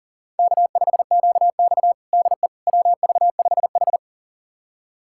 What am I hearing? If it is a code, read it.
Morse code, heard as a keyed tone, about 40 wpm, 700 Hz. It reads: K5QX DE WV5H
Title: USA Callsign DE Callsign - 4 Characters - 40wpm